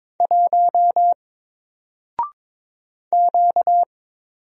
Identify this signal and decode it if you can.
Morse code, heard as a keyed tone, about 22 wpm, 700 Hz. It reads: 1 Q
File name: Single-Letter-Number-22wpm